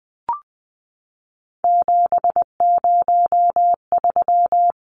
Morse code, heard as a keyed tone, about 20 wpm, 700 Hz. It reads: 703
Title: Numbers - 3-Digit 20wpm